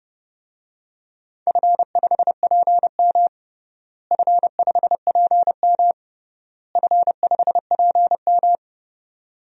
Morse code, heard as a keyed tone, about 29 wpm, 700 Hz. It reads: F5PM F5PM F5PM
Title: DXCC-100 Call Signs - 4 Characters - Repeated 3x - 30wpm